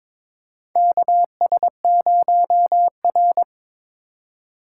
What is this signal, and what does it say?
Morse code, heard as a keyed tone, about 22 wpm, 700 Hz. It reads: KS0R
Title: USA Call Signs - 2x Character Spacing - 22wpm